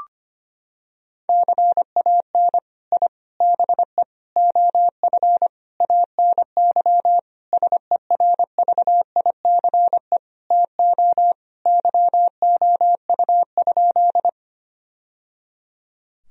Morse code, heard as a keyed tone, about 25 wpm, 700 Hz. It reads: CAN I BE OF ANY SERVICE TO YOU?
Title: Sentences from Top 1000 Words - Review Part 4 - 25wpm